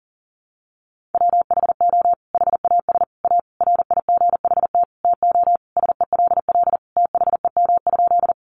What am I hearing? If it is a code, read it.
Morse code, heard as a keyed tone, about 40 wpm, 700 Hz. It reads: WHO HAS A RIGHT TO SELL THEM?
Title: Sentences from Top 1100 Words 40wpm